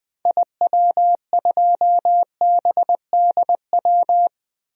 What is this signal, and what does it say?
Morse code, heard as a keyed tone, about 20 wpm, 700 Hz. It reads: IW2BDW
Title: DXCC-100 Call Signs - 6 Characters - Repeated 3x - 20wpm